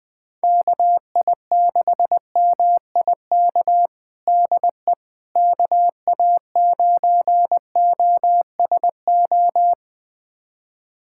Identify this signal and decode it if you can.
Morse code, heard as a keyed tone, about 20 wpm, 700 Hz. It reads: KI6MIK DE KA9OSO
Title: USA Callsign DE Callsign - 6 Characters - 20wpm